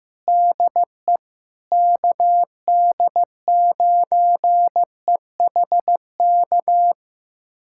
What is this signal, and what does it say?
Morse code, heard as a keyed tone, about 15 wpm, 700 Hz. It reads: DE KD9EHK
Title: USA Callsign DE Callsign - 6 Characters - 15wpm